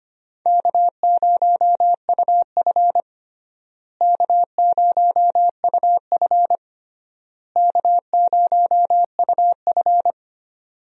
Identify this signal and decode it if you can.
Morse code, heard as a keyed tone, about 25 wpm, 700 Hz. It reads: K0UF K0UF K0UF
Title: USA Call Signs - 4 Characters - Repeated 3x - 25wpm